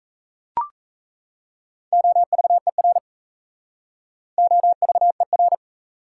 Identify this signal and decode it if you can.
Morse code, heard as a keyed tone, about 34 wpm, 700 Hz. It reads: OVER OVER
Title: Speed-Racing Prefixes and Suffixes 28wpm